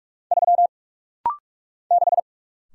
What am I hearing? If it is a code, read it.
Morse code, heard as a keyed tone, about 46 wpm, 700 Hz. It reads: 3 6
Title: Single Letter-Number - Rapid-Fire 45wpm